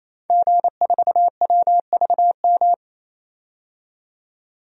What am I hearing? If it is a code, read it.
Morse code, heard as a keyed tone, about 28 wpm, 700 Hz. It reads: G4WVM
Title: DXCC-100 Call Signs - 5 Characters - 28wpm